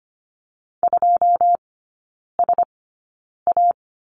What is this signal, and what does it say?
Morse code, heard as a keyed tone, about 25 wpm, spaced as for 9 wpm, 690 Hz. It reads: 2SA
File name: Lesson 15e - Number 5 - Three Characters 25wpm